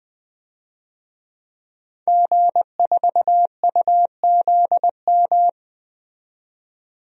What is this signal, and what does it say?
Morse code, heard as a keyed tone, about 20 wpm, 700 Hz. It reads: G4UZM
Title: DXCC-100 Call Signs - 5 Characters - 20wpm